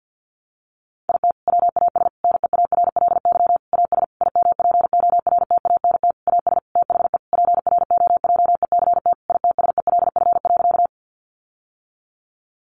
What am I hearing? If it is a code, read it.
Morse code, heard as a keyed tone, about 50 wpm, 700 Hz. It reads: IT WAS NEARLY AS IMPORTANT AS THE PROJECT ITSELF.
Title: Sentences from Top 1400 Words 50wpm